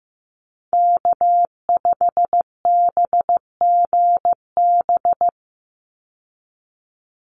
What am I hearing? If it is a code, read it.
Morse code, heard as a keyed tone, about 15 wpm, 700 Hz. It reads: K5BGB
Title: Call Signs Repeated 3x - 15wpm